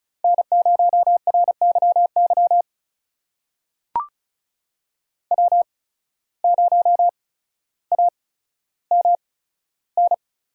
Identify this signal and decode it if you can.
Morse code, heard as a keyed tone, about 36 wpm, spaced as for 15 wpm, 700 Hz. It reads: N0RYY W0AMN
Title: USA Call Signs - 5 Characters - 8x Character Spacing - 35wpm